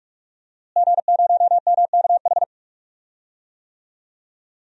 Morse code, heard as a keyed tone, about 42 wpm, 700 Hz. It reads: G0MKH
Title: DXCC-100 Call Signs - 5 Characters - Repeated 2x - 45wpm